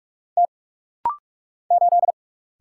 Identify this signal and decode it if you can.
Morse code, heard as a keyed tone, about 44 wpm, 690 Hz. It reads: T 8
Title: Single Letter-Number - Rapid-Fire 45wpm